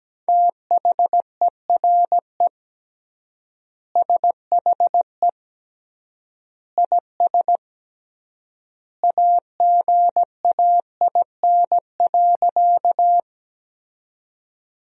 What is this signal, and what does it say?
Morse code, heard as a keyed tone, about 17 wpm, 700 Hz. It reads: THERE SHE IS AGAIN.
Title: 3x Word-Spacing - Sentences from Top 200 Words - 17wpm